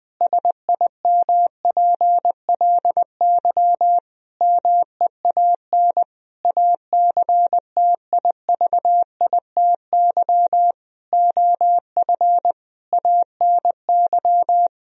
Morse code, heard as a keyed tone, about 20 wpm, 700 Hz. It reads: SIMPLY MEAN ACTIVITY OF ANY
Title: Sentences from Top 1000 Words - Review Part 4 - 20wpm